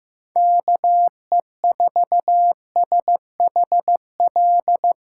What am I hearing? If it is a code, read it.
Morse code, heard as a keyed tone, about 15 wpm, 700 Hz. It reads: KE4SHL
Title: USA Callsign DE Callsign - 6 Characters - 15wpm